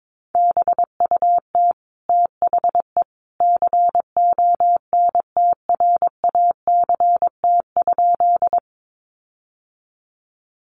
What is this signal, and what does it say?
Morse code, heard as a keyed tone, about 22 wpm, 700 Hz. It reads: BUT THE CONTRACT?
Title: Sentences from Top 1400 Words 22wpm